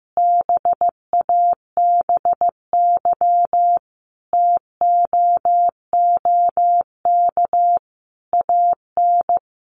Morse code, heard as a keyed tone, about 15 wpm, 700 Hz. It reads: BABY TOOK AN
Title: Sentences from Top 1300 Words 15wpm